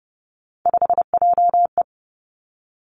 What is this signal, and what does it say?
Morse code, heard as a keyed tone, about 30 wpm, 700 Hz. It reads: 5JE
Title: Three Challenging Letter-Number Combinations 30wpm